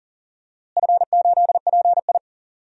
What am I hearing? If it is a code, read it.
Morse code, heard as a keyed tone, about 40 wpm, 700 Hz. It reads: F8PI
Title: DXCC-100 Call Signs - 4 Characters - Repeated 2x - 40wpm